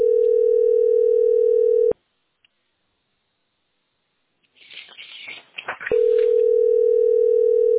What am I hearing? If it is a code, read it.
Ringback tone (North America), 440 + 480 Hz.